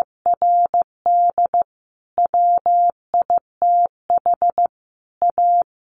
Morse code, heard as a keyed tone, about 15 wpm, 700 Hz. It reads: RD WITH A
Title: A Sound Of Thunder 15wpm